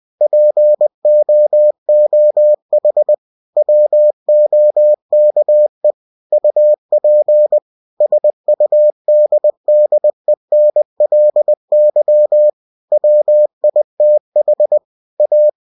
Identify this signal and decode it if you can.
Morse code, heard as a keyed tone, about 20 wpm, 600 Hz. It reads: POOH WOKE UP SUDDENLY WITH A
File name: cw-books / winnie-the-pooh / winnie-20wpm-page-0038